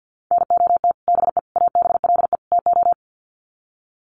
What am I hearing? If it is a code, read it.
Morse code, heard as a keyed tone, about 50 wpm, 700 Hz. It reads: NOT BE ABLE TO
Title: Sets of 4 Words 50wpm